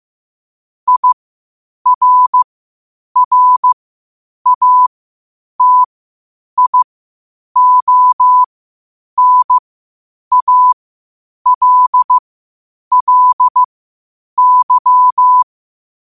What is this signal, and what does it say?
Morse code, heard as a keyed tone, about 15 wpm, spaced as for 8 wpm, 1000 Hz. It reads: IRRATIONALLY
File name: qotdfiles / 8wpm